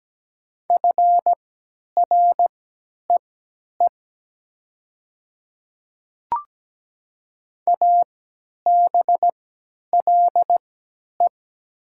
Morse code, heard as a keyed tone, about 17 wpm, spaced as for 10 wpm, 690 Hz. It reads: FREE ABLE
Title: 3x Character Spacing - Top 500 Words - 17wpm